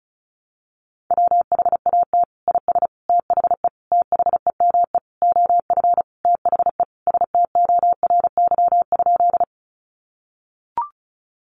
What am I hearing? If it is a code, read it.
Morse code, heard as a keyed tone, about 34 wpm, 700 Hz. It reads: WHAT IS THE THEME OF THE STORY?
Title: Sentences from Top 1100 Words 35wpm